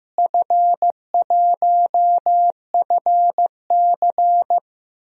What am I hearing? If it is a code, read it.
Morse code, heard as a keyed tone, about 15 wpm, 700 Hz. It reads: F1FC
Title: DXCC-100 Call Signs - 4 Characters - Repeated 2x - 15wpm